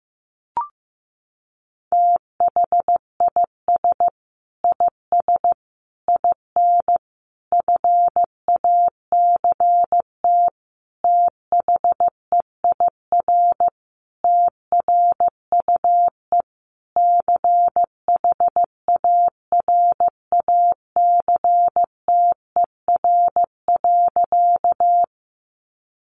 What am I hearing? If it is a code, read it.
Morse code, heard as a keyed tone, about 15 wpm, 700 Hz. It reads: THIS IS IN FACT THEIR TRUE CHARACTER.
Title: Sentences from Top 1200 Words 15wpm